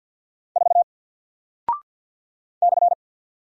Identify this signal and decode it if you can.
Morse code, heard as a keyed tone, about 48 wpm, 700 Hz. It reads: 4 /
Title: Single Character (Letter:Number:Punctuation:Prosign) Alt 50wpm